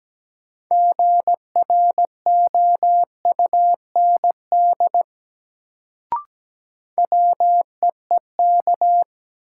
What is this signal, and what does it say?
Morse code, heard as a keyed tone, about 17 wpm, 700 Hz. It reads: GROUND WEEK
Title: Sentences from Top 500 Words - New Word Drill - 17wpm